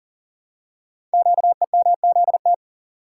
Morse code, heard as a keyed tone, about 41 wpm, 690 Hz. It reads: QEMZT
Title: Code Groups - 5 Letters - 40wpm